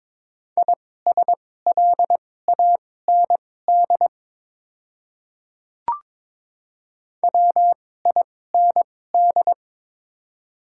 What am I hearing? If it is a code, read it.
Morse code, heard as a keyed tone, about 22 wpm, spaced as for 16 wpm, 700 Hz. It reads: ISLAND WIND
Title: 2x Character Spacing - Top 500 Words - 22wpm